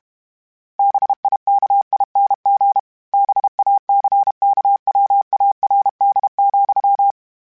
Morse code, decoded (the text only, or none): BIKING BACKWARD,